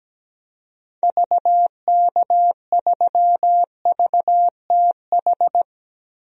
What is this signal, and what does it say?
Morse code, heard as a keyed tone, about 17 wpm, 700 Hz. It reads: VK3VTH
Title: DXCC-100 Call Signs - 6 Characters - 17wpm